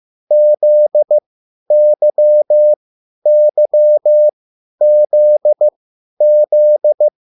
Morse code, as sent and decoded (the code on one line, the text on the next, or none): --.. -.-- -.-- --.. --..
ZYYZZ